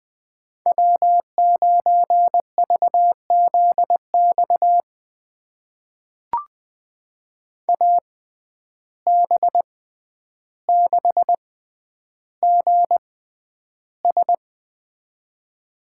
Morse code, heard as a keyed tone, about 20 wpm, spaced as for 10 wpm, 700 Hz. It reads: W9VZX AB6GS
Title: USA Call Signs - 6x Character Spacing - 20wpm